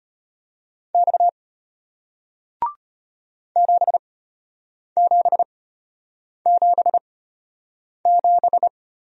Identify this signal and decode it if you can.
Morse code, heard as a keyed tone, about 34 wpm, spaced as for 8 wpm, 700 Hz. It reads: X 7777